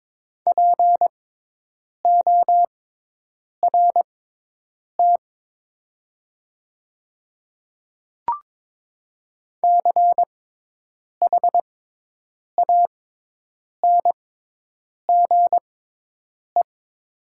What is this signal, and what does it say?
Morse code, heard as a keyed tone, about 22 wpm, spaced as for 8 wpm, 700 Hz. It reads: PORT CHANGE
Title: 6x Character Spacing - Top 200 Words - 22wpm